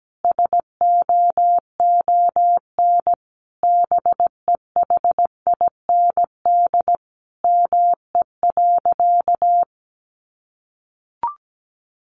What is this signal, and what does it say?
Morse code, heard as a keyed tone, about 17 wpm, 700 Hz. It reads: SOON BEHIND ME.
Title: Sentences from Top 500 Words 17wpm